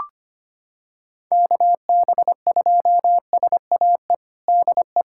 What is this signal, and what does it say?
Morse code, heard as a keyed tone, about 25 wpm, 700 Hz. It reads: KB2SAE DE
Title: USA Callsign DE Callsign - 6 Characters - 25wpm